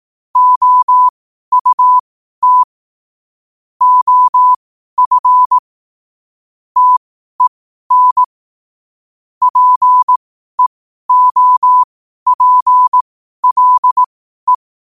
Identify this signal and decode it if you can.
Morse code, heard as a keyed tone, about 18 wpm, spaced as for 12 wpm, 1000 Hz. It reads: OUT OF TEN PEOPLE
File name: qotdfiles / 12wpm